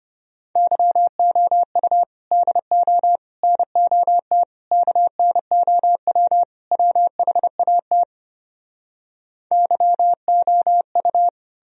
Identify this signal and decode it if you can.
Morse code, heard as a keyed tone, about 33 wpm, 700 Hz. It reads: YOU DO NOT KNOW WHAT YOU